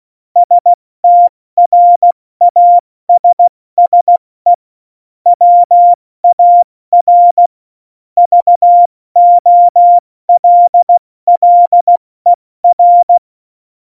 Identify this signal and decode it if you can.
Morse code, heard as a keyed tone, about 16 wpm, 700 Hz. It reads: STRASSE WAR VOLLER
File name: Morsekurs / Witze / Witz 05. Januar 2021, 70 F80